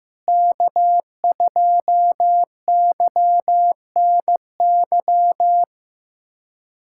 Morse code, heard as a keyed tone, about 16 wpm, 700 Hz. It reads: K2YNY
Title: USA Callsign DE Callsign - 5 Characters - 15wpm